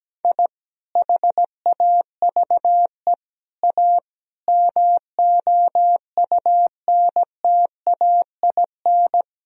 Morse code, heard as a keyed tone, about 17 wpm, 700 Hz. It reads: I HAVE A MOUNTAIN
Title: Sentences from Top 1000 Words - Review Part 1 - 17wpm